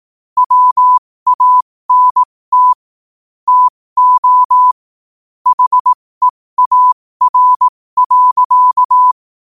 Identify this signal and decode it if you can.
Morse code, heard as a keyed tone, about 18 wpm, spaced as for 15 wpm, 1000 Hz. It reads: WANT TO HEAR.
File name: qotdfiles / 15wpm